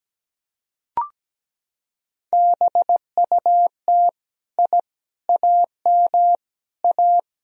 Morse code, heard as a keyed tone, about 17 wpm, 700 Hz. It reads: BUT I AM A
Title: Sentences from Top 1000 Words - Review Part 3 - 17wpm